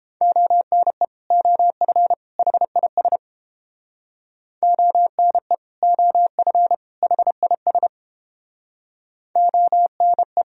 Morse code, heard as a keyed tone, about 30 wpm, 700 Hz. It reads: ONE OF HIS ONE OF HIS ONE